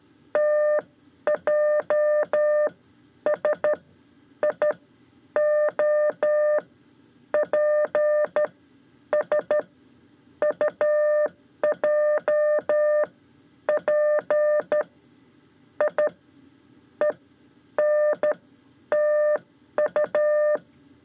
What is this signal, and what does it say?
Morse code, heard as a keyed tone, about 12 wpm, spaced as for 10 wpm, 600 Hz. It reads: TJS I O P S UJ P I E N TU
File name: 5 Lezione CW 2